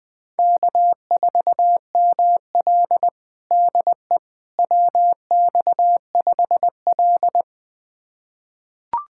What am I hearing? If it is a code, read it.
Morse code, heard as a keyed tone, about 21 wpm, 700 Hz. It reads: K4ML DE WX5L
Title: USA Callsign DE Callsign - 4 Characters - 20wpm